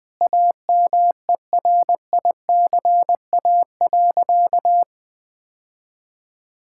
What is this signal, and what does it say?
Morse code, heard as a keyed tone, about 21 wpm, 700 Hz. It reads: AMERICA.
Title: Sentences with 5-Word Phrases - 20wpm